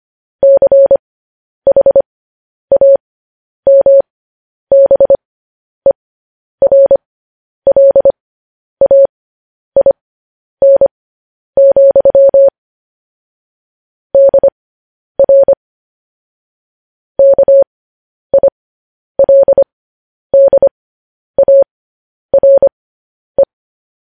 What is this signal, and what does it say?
Morse code, heard as a keyed tone, about 25 wpm, spaced as for 10 wpm, 550 Hz. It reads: CHAMBERLAIN, DR KILDARE
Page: news-headlines-in-morse-code-at-10-wpm